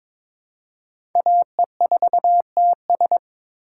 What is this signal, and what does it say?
Morse code, heard as a keyed tone, about 23 wpm, 700 Hz. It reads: AE4TS
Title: USA Call Signs - 5 Characters - 2x Character Spacing - 22wpm